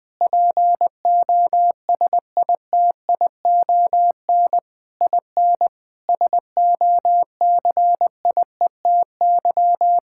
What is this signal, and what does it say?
Morse code, heard as a keyed tone, about 19 wpm, 700 Hz. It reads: POSITION IN SOCIETY
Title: Sentences from Top 1000 Words - Review Part 5 - 20wpm